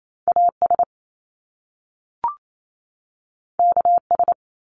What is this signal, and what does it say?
Morse code, heard as a keyed tone, about 28 wpm, 700 Hz. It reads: AS KS
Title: Top Two Letter Combinations in Words 28wpm